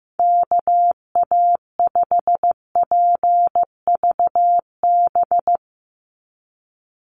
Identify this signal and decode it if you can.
Morse code, heard as a keyed tone, about 15 wpm, 700 Hz. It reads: KA5PVB
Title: NA SOTA Chaser Callsigns 15wpm